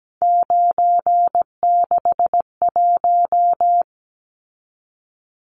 Morse code, heard as a keyed tone, about 17 wpm, 700 Hz. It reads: 961
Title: Numbers - 3-Digit 17wpm